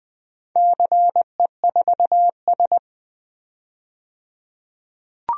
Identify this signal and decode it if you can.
Morse code, heard as a keyed tone, about 20 wpm, 700 Hz. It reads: CE4S
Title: DXCC-100 Call Signs - 4 Characters - 20wpm